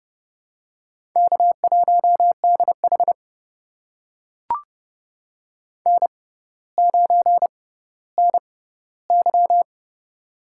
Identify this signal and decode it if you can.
Morse code, heard as a keyed tone, about 30 wpm, spaced as for 15 wpm, 700 Hz. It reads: K1DH N9NY
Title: USA Call Signs - 4 Characters - 6x Character Spacing - 30wpm